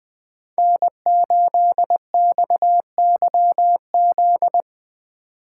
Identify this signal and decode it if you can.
Morse code, heard as a keyed tone, about 20 wpm, 700 Hz. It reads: N8XYZ
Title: Speed Racing - USA 1x3 Call Signs 17wpm